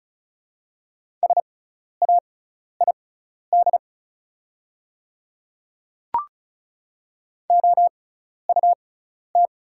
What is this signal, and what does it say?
Morse code, heard as a keyed tone, about 34 wpm, spaced as for 12 wpm, 700 Hz. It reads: SAID OUT
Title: 6x Character Spacing - Top 100 Words - 35wpm